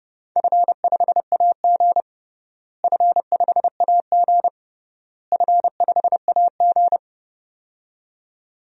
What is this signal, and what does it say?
Morse code, heard as a keyed tone, about 32 wpm, 700 Hz. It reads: F5AG F5AG F5AG
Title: DXCC-100 Call Signs - 4 Characters - Repeated 3x - 30wpm